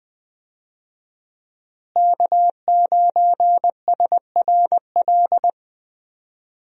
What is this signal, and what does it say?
Morse code, heard as a keyed tone, about 20 wpm, 700 Hz. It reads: K9SRL
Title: USA Call Signs - 5 Characters - Repeated 2x - 20wpm